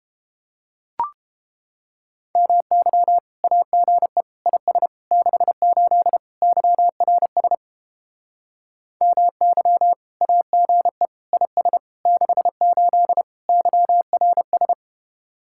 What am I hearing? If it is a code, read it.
Morse code, heard as a keyed tone, about 29 wpm, 700 Hz. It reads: MY AGE IS 68 YRS MY AGE IS 68 YRS